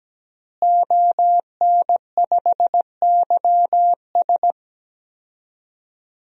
Morse code, heard as a keyed tone, about 17 wpm, 700 Hz. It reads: ON5YS
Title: DXCC-100 Call Signs - 5 Characters - Repeated 3x - 17wpm